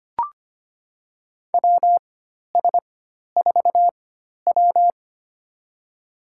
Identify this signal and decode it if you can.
Morse code, heard as a keyed tone, about 25 wpm, spaced as for 12 wpm, 690 Hz. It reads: WS4W
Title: USA Call Signs - 4 Characters - 4x Character Spacing - 25wpm